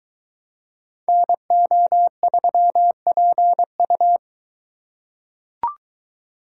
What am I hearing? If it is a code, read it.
Morse code, heard as a keyed tone, about 20 wpm, 700 Hz. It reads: NO3PU